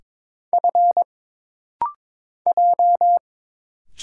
Morse code, heard as a keyed tone, about 22 wpm, 700 Hz. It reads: F J
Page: This rapid-fire timing is geared to push your ICR (Instant Character Recognition) skills.